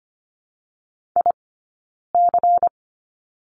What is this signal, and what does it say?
Morse code, heard as a keyed tone, about 25 wpm, 700 Hz. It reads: I C
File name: course / 25:10wpm / 092 Lesson 16d - Letter c - Two Characters 25wpm